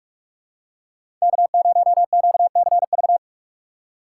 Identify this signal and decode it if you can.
Morse code, heard as a keyed tone, about 46 wpm, 700 Hz. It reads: K0QCV
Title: USA Call Signs - 5 Characters - 3x Character Spacing - 45wpm